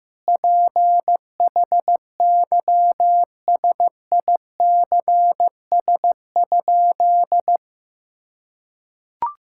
Morse code, heard as a keyed tone, about 15 wpm, 700 Hz. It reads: PHYSICS?
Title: Sentences from Top 1000 Words - Review Part 5 - 15wpm